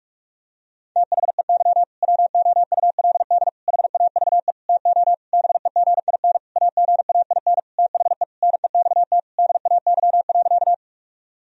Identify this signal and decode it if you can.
Morse code, heard as a keyed tone, about 41 wpm, 700 Hz. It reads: THEY WOULD HAVE TO BEGIN AGAIN THE NEXT DAY.
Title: Sentences from Top 1000 Words - Review Part 1 - 45wpm